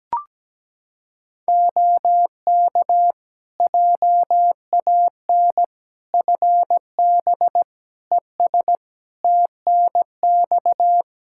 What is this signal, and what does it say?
Morse code, heard as a keyed tone, about 17 wpm, 700 Hz. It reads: OK JAN FB ES TNX
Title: K1USN SST QSO Practice 17wpm